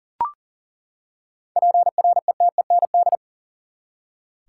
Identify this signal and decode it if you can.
Morse code, heard as a keyed tone, about 40 wpm, 700 Hz. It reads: PRETEND
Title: Seven Letter Words - 40wpm